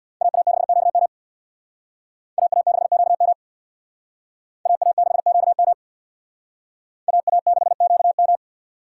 Morse code, heard as a keyed tone, about 67 wpm, 700 Hz. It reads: AA6QM AA6QM AA6QM AA6QM